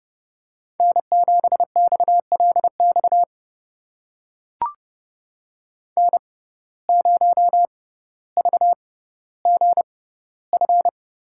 Morse code, heard as a keyed tone, about 30 wpm, spaced as for 15 wpm, 700 Hz. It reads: N7XLX N0VGF
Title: USA Call Signs - 5 Characters - 6x Character Spacing - 30wpm